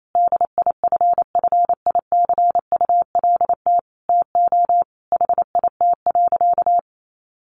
Morse code, heard as a keyed tone, about 28 wpm, 700 Hz. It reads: DIFFICULT TO HIT.
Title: Sentences from Top 1200 Words 28wpm